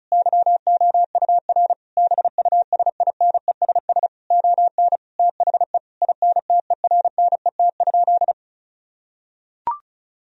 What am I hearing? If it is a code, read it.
Morse code, heard as a keyed tone, about 34 wpm, 700 Hz. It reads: YOUR BUSINESS ON THE INTERNET?
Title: Sentences from Top 1000 Words - Review Part 4 - 35wpm